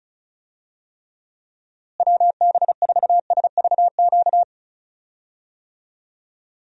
Morse code, heard as a keyed tone, about 36 wpm, 700 Hz. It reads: WB4SVQ
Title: USA Call Signs - 6 Characters - Repeated 3x - 35wpm